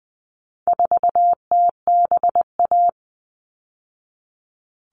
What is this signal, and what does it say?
Morse code, heard as a keyed tone, about 20 wpm, 700 Hz. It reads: 4TBA
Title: All Cut Numbers - 4-Digit 20wpm